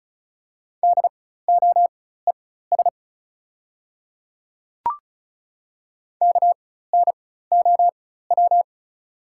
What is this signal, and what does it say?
Morse code, heard as a keyed tone, about 36 wpm, spaced as for 16 wpm, 690 Hz. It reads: DOES KNOW
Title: 4x Character Spacing - Top 200 Words - 35wpm